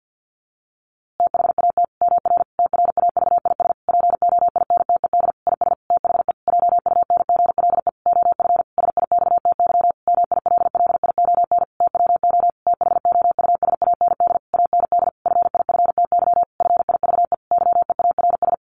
Morse code, heard as a keyed tone, about 50 wpm, 700 Hz. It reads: THAT MR TRAVIS POINTED IS THE JUNGLE OF SIXTY MILLION TWO THOUSAND AND FIFTY FIVE YEARS
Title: A Sound Of Thunder 50wpm